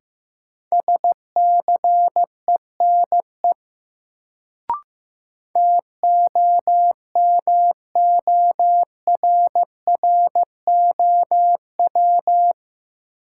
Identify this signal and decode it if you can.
Morse code, heard as a keyed tone, about 15 wpm, 700 Hz. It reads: SCENE TOMORROW
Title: Sentences from Top 1100 Words - New Word Drill - 15wpm